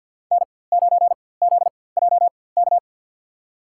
Morse code, heard as a keyed tone, about 53 wpm, spaced as for 24 wpm, 700 Hz. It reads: N9ZJK
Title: USA Call Signs - 5 Characters - 4x Character Spacing - 50wpm